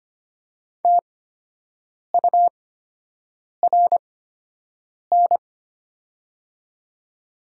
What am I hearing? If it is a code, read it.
Morse code, heard as a keyed tone, about 25 wpm, spaced as for 7 wpm, 700 Hz. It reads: TURN
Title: 8x Character Spacing - Top 200 Words - 25wpm